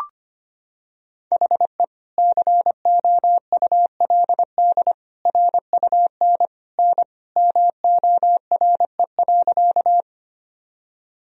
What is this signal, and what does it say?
Morse code, heard as a keyed tone, about 25 wpm, 700 Hz. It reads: HE COULD RUN N MORE.
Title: Sentences from Top 300 Words and Abbreviations 25wpm